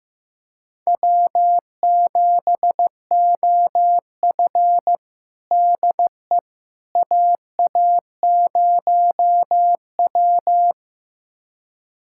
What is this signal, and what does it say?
Morse code, heard as a keyed tone, about 16 wpm, 700 Hz. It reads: W7OF DE AA0W
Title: USA Callsign DE Callsign - 4 Characters - 15wpm